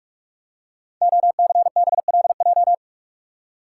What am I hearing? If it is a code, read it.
Morse code, heard as a keyed tone, about 44 wpm, 690 Hz. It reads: OCBLJ
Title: Code Groups - 5 Letters - 45wpm